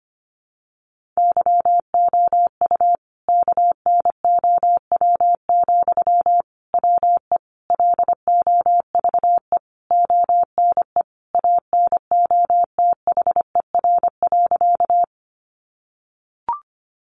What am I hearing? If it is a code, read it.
Morse code, heard as a keyed tone, about 25 wpm, 700 Hz. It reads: YOU KNOW, WE LOVE ONE ANOTHER.
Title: Sentences from Top 600 Words 25wpm